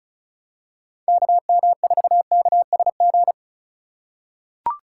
Morse code, heard as a keyed tone, about 36 wpm, 700 Hz. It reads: KM4KSG
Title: USA Call Signs - 3x Character Spacing - 35wpm